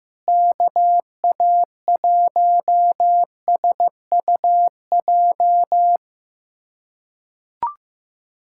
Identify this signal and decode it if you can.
Morse code, heard as a keyed tone, about 15 wpm, 700 Hz. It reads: KA1SUJ
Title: USA Callsign DE Callsign 15wpm